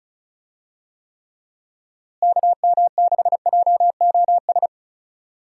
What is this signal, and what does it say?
Morse code, heard as a keyed tone, about 34 wpm, 700 Hz. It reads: KM6JOS
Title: USA Call Signs - 6 Characters - Repeated 3x - 35wpm